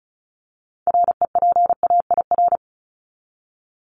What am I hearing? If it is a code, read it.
Morse code, heard as a keyed tone, about 34 wpm, 700 Hz. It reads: REPAIR
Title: Top 1501-1750 Words 35wpm